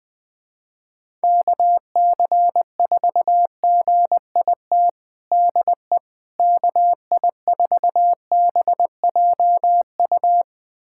Morse code, heard as a keyed tone, about 20 wpm, 700 Hz. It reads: KC4GIT DE KI4BJU
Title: USA Callsign DE Callsign 20wpm